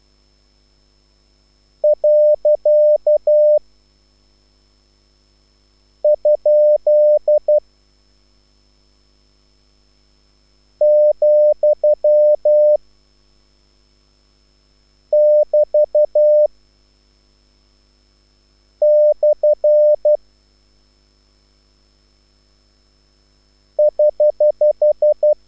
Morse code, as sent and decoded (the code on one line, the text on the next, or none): .-.-.- / ..--.. / --..-- / -...- / -..-. / ........
. ? , <BT> / <HH>